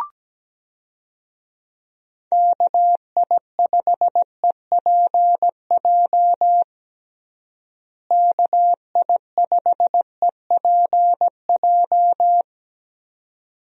Morse code, heard as a keyed tone, about 17 wpm, 700 Hz. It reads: KI5EPJ KI5EPJ
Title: USA Call Signs - 6 Characters - Repeated 2x - 17wpm